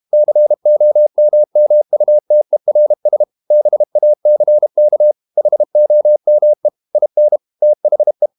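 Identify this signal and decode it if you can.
Morse code, heard as a keyed tone, about 32 wpm, 600 Hz. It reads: COMMUTERS BACK HOME IN THE
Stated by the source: Morse code practice at 32 wpm from CW med Gnister, reading Ken Follett's novel Never in English.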